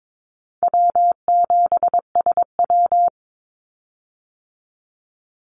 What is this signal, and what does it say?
Morse code, heard as a keyed tone, about 22 wpm, 700 Hz. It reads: W7SW
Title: Call Signs Repeated 2x - 22wpm